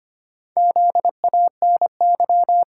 Morse code, heard as a keyed tone, about 25 wpm, 700 Hz. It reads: ZANY
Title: Pangram Challenge v2 - 25wpm